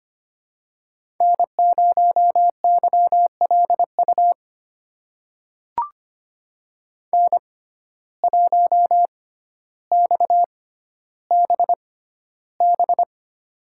Morse code, heard as a keyed tone, about 25 wpm, spaced as for 13 wpm, 700 Hz. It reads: N0YLU N1XBB
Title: USA Call Signs - 5 Characters - 6x Character Spacing - 25wpm